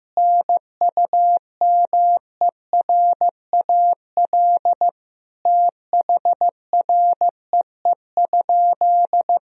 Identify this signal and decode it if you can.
Morse code, heard as a keyed tone, about 16 wpm, 700 Hz. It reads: NUMERAL THREE?
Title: Sentences from Top 1000 Words - Review Part 2 - 15wpm